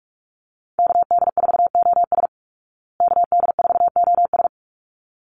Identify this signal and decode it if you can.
Morse code, heard as a keyed tone, about 44 wpm, 700 Hz. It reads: KD4OS KD4OS
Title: Call Signs Repeated 3x - 45wpm